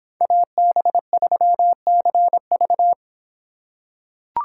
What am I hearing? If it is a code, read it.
Morse code, heard as a keyed tone, about 24 wpm, 700 Hz. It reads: AB3CV